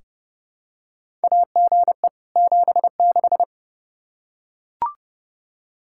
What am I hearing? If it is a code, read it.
Morse code, heard as a keyed tone, about 30 wpm, 700 Hz. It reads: AGE 76
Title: K1USN SST QSO Practice 30wpm